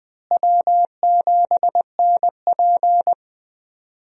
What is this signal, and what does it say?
Morse code, heard as a keyed tone, about 20 wpm, 700 Hz. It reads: W7NP
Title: USA Call Signs - 4 Characters - Repeated 3x - 20wpm